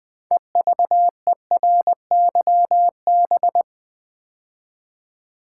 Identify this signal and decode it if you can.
Morse code, heard as a keyed tone, about 20 wpm, 700 Hz. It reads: EVERYB
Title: Build A Word - 3 Letters At A Time - 9-Letter Words - 20wpm